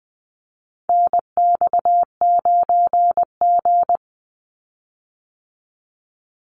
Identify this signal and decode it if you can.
Morse code, heard as a keyed tone, about 20 wpm, 700 Hz. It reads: NX9G
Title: Call Signs Repeated 3x - 20wpm